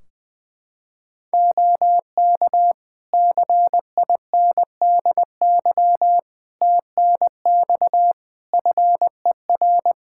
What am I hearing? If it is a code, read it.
Morse code, heard as a keyed tone, about 20 wpm, 700 Hz. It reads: OK CINDY TNX FER
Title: K1USN SST QSO Practice 20wpm